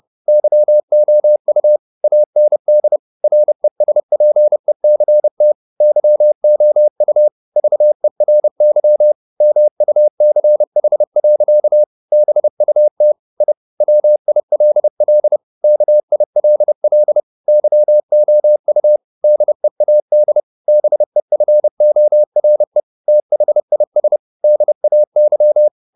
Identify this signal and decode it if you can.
Morse code, heard as a keyed tone, about 30 wpm, 600 Hz. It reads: YOU AND RESPECT YOU VERY MUCH. BUT I WILL KILL YOU DEAD BEFORE THIS DAY
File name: the old man and the sea / the old man and the sea 1 30wpm